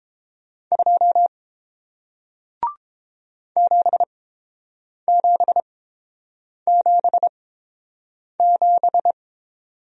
Morse code, heard as a keyed tone, about 31 wpm, spaced as for 7 wpm, 700 Hz. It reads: 2 7777